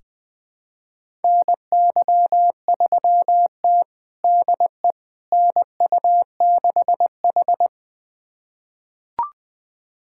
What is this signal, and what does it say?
Morse code, heard as a keyed tone, about 20 wpm, 700 Hz. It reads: NY3T DE NU6H
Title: USA Callsign DE Callsign - 4 Characters - 20wpm